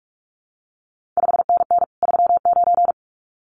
Morse code, heard as a keyed tone, about 46 wpm, 700 Hz. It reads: 5NN 39
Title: CQ WW DX Contest Exchange Encore - 45wpm